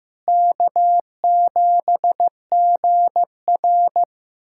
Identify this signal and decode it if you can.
Morse code, heard as a keyed tone, about 15 wpm, 700 Hz. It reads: K7GR
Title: USA Callsign DE Callsign - 4 Characters - 15wpm